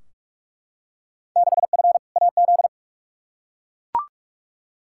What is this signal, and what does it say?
Morse code, heard as a keyed tone, about 43 wpm, 700 Hz. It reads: 6F AZ
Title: Field Day Exchanges Encore - 45wpm